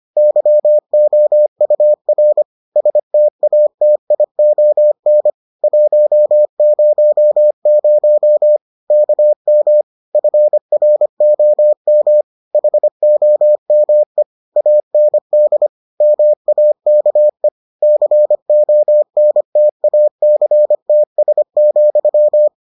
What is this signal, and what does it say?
Morse code, heard as a keyed tone, about 25 wpm, 600 Hz. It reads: YOUR STATION 100 KM FROM HOME AND MAKE CONTACTS,